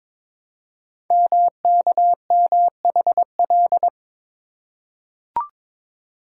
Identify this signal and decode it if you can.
Morse code, heard as a keyed tone, about 22 wpm, 700 Hz. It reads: MKMHL
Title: Code Groups - 5 Letters - 22wpm